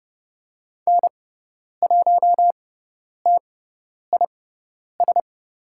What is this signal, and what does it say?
Morse code, heard as a keyed tone, about 30 wpm, spaced as for 10 wpm, 690 Hz. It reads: N1TIS
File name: Lesson 17g - Letter u - Call Signs 30wpm